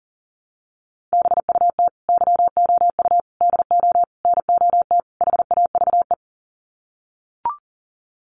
Morse code, heard as a keyed tone, about 40 wpm, 700 Hz. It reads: BUT YOU DO NOT HAVE
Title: Sets of 5 Words Encore 40wpm